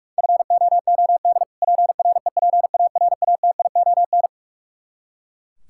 Morse code, heard as a keyed tone, about 42 wpm, 700 Hz. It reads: FOOD PREPARATION
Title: Practice QSO - Exchange 3 - More Info - 45wpm